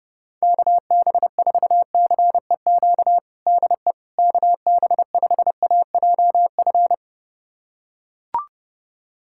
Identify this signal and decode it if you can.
Morse code, heard as a keyed tone, about 30 wpm, 700 Hz. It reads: KB4CEQ DE KB5AJF
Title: USA Callsign DE Callsign 30wpm